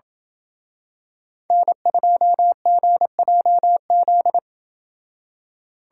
Morse code, heard as a keyed tone, about 28 wpm, 700 Hz. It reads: N2GJZ